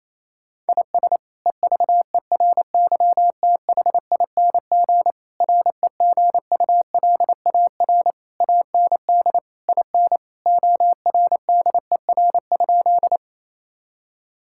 Morse code, heard as a keyed tone, about 28 wpm, 700 Hz. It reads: IS EVERYTHING REGULAR AND IN ORDER?
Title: Sentences from Top 1000 Words - Review Part 6 - 28wpm